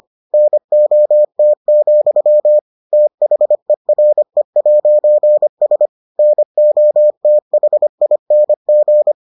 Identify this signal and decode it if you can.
Morse code, heard as a keyed tone, about 25 wpm, 600 Hz. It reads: NOT, THERE'S NOTHING